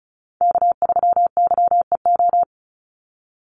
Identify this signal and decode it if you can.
Morse code, heard as a keyed tone, about 34 wpm, 700 Hz. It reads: K3YEO
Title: Call Signs Repeated 2x - 35wpm